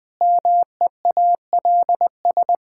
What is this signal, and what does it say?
Morse code, heard as a keyed tone, about 21 wpm, 700 Hz. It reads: MEALS
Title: Pangram Challenge v2 - 20wpm